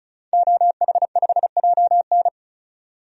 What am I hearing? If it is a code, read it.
Morse code, heard as a keyed tone, about 36 wpm, 700 Hz. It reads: OH5JN
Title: DXCC-100 Call Signs - 5 Characters - 35wpm